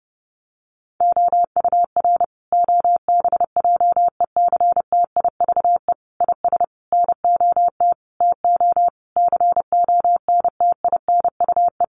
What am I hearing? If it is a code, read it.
Morse code, heard as a keyed tone, about 30 wpm, 700 Hz. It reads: OUR OBJECTIVE IS NOT TO CONTINUE
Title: Sentences from Top 1200 Words 30wpm